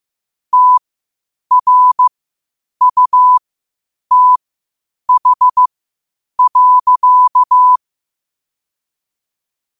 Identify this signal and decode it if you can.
Morse code, heard as a keyed tone, about 15 wpm, spaced as for 8 wpm, 1000 Hz. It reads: TRUTH.
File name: qotdfiles / 8wpm